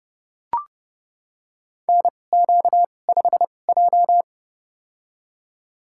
Morse code, heard as a keyed tone, about 30 wpm, spaced as for 21 wpm, 700 Hz. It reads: NQ5J
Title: USA Call Signs - 4 Characters - 2x Character Spacing - 30wpm